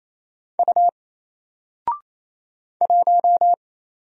Morse code, heard as a keyed tone, about 27 wpm, 700 Hz. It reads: U 1
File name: Single-Letter-Number-28wpm